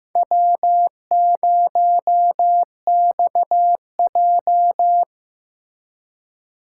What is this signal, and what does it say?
Morse code, heard as a keyed tone, about 18 wpm, 700 Hz. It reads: W0XJ
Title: Speed Racing - USA 1x2 Call Signs 15wpm